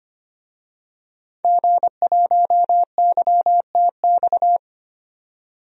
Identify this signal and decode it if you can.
Morse code, heard as a keyed tone, about 25 wpm, 700 Hz. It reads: G1YTX
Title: DXCC-100 Call Signs - 5 Characters - 25wpm